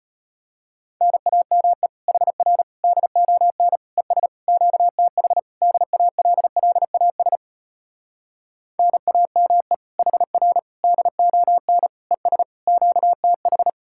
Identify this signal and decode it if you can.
Morse code, heard as a keyed tone, about 33 wpm, 700 Hz. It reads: NAME HR DON ES QTH DALLAS NAME HR DON ES QTH